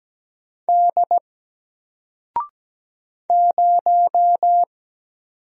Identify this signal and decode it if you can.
Morse code, heard as a keyed tone, about 17 wpm, 700 Hz. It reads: D 0
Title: Single Letter-Number Alt 17wpm